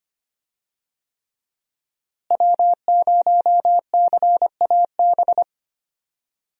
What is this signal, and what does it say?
Morse code, heard as a keyed tone, about 25 wpm, 700 Hz. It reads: W0CAB
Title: USA Call Signs - 5 Characters - Repeated 3x - 25wpm